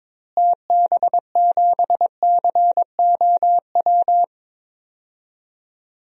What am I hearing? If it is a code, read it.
Morse code, heard as a keyed tone, about 22 wpm, 700 Hz. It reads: TB7COW
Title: DXCC-100 Call Signs - 6 Characters - Repeated 2x - 22wpm